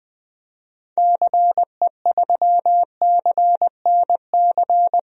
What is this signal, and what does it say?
Morse code, heard as a keyed tone, about 20 wpm, 700 Hz. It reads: CE3CNC
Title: DXCC-100 Call Signs - 6 Characters - Repeated 3x - 20wpm